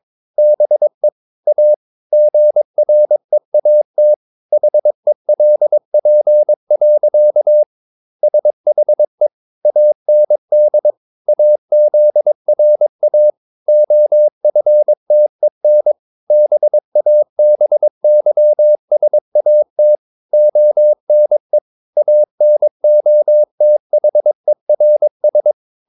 Morse code, heard as a keyed tone, about 22 wpm, 600 Hz. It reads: BE A GREAT HELP. SHE AND AZRA OFTEN BABYSAT ONE ANOTHERS
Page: Romanen Never af Ken Follet på engelsk.